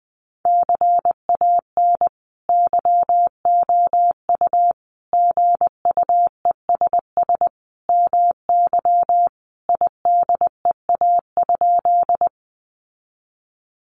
Morse code, heard as a keyed tone, about 20 wpm, 700 Hz. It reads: CAN YOU GUESS MY IDEA?
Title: Sentences from Top 1500 Words 20wpm